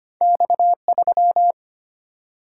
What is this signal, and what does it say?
Morse code, heard as a keyed tone, about 25 wpm, 700 Hz. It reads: X3
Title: Single-Letter Single-Number 25wpm